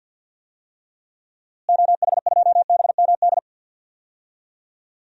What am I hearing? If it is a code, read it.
Morse code, heard as a keyed tone, about 51 wpm, 700 Hz. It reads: OH2BMD
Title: DXCC-100 Call Signs - 6 Characters - Repeated 3x - 50wpm